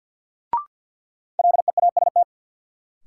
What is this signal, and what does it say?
Morse code, heard as a keyed tone, about 53 wpm, 700 Hz. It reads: LEAST
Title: Sets of 2 Words - New Word Drill - 50wpm